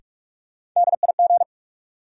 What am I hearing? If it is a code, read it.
Morse code, heard as a keyed tone, about 43 wpm, 700 Hz. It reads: DIG
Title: Sentences from Top 1600 Words - New Word Drill - 45wpm